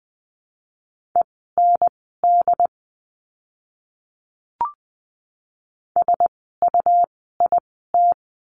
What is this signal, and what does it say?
Morse code, heard as a keyed tone, about 20 wpm, spaced as for 15 wpm, 700 Hz. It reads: END SUIT
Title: 101 Lesson 17f - Letter u - Words 20wpm